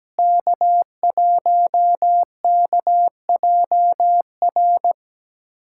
Morse code, heard as a keyed tone, about 17 wpm, 700 Hz. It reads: K1KJR
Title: USA Callsign DE Callsign 17wpm